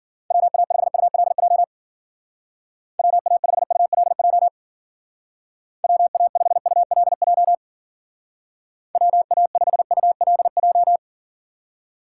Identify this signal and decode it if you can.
Morse code, heard as a keyed tone, about 49 wpm, 700 Hz. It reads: WA5ULJ WA5ULJ WA5ULJ WA5ULJ
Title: Speed Racing - USA 2x3 Call Signs 40wpm